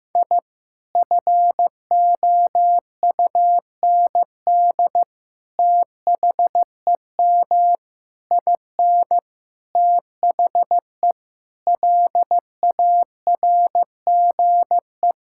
Morse code, heard as a keyed tone, about 15 wpm, 700 Hz. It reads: I FOUND THEM IN THE LARGE
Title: Sentences from Top 1000 Words - Review Part 5 - 15wpm